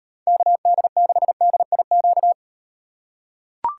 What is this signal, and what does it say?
Morse code, heard as a keyed tone, about 38 wpm, 700 Hz. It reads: KD6DIQ